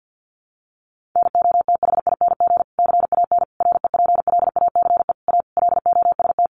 Morse code, heard as a keyed tone, about 50 wpm, 700 Hz. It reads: NOTHING CAN REPLACE A LOST
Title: Sentences from Top 1400 Words 50wpm